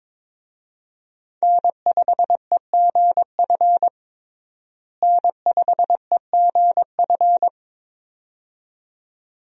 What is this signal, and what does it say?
Morse code, heard as a keyed tone, about 22 wpm, 700 Hz. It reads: N5EGF N5EGF
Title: USA Call Signs - 5 Characters - Repeated 2x - 22wpm